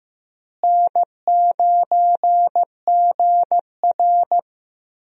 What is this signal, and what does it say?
Morse code, heard as a keyed tone, about 15 wpm, 700 Hz. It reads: N9GR
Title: USA Call Signs - 4 Characters - Repeated 3x - 15wpm